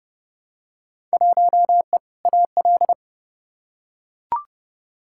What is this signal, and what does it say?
Morse code, heard as a keyed tone, about 30 wpm, 700 Hz. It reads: 1E AL
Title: Field Day Exchanges - 30wpm